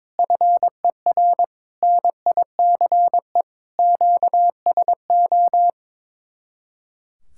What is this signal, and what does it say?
Morse code, heard as a keyed tone, about 22 wpm, 700 Hz. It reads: FER NICE QSO
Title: Speed-Racing Common QSO Phrases - 22wpm